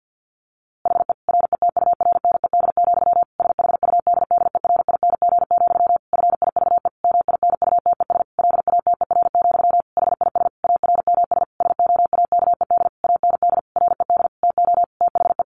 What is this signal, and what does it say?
Morse code, heard as a keyed tone, about 50 wpm, 700 Hz. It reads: HE RETURNED, SHUDDERING, FIVE MINUTES LATER, HIS ARMS SOAKED AND RED TO THE
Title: A Sound Of Thunder 50wpm